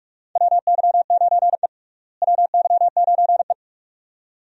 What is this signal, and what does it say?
Morse code, heard as a keyed tone, about 42 wpm, 700 Hz. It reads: WY9E WY9E
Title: USA Call Signs - 4 Characters - Repeated 2x - 45wpm